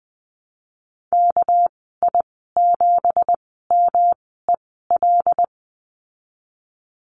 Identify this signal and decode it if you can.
Morse code, heard as a keyed tone, about 20 wpm, spaced as for 15 wpm, 700 Hz. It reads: KI7MEL
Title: 213 Lesson 34g - Number 8 - Call Signs 20wpm